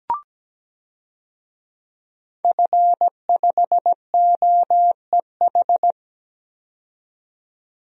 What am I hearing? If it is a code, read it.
Morse code, heard as a keyed tone, about 17 wpm, 700 Hz. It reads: F5OEH
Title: DXCC-100 Call Signs - 5 Characters - 17wpm